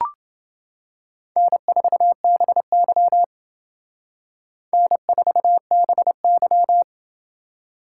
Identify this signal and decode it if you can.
Morse code, heard as a keyed tone, about 28 wpm, 700 Hz. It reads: N4BY N4BY